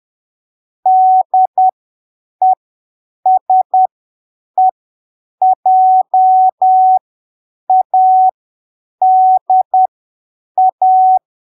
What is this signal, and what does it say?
Morse code, heard as a keyed tone, about 10 wpm, spaced as for 7 wpm, 750 Hz. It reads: DESEJADA
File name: arquivos / texto / texto 04 10ppm